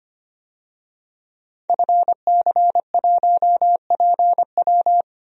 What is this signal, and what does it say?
Morse code, heard as a keyed tone, about 25 wpm, 700 Hz. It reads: FC1PW
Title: DXCC-100 Call Signs - 5 Characters - Repeated 3x - 25wpm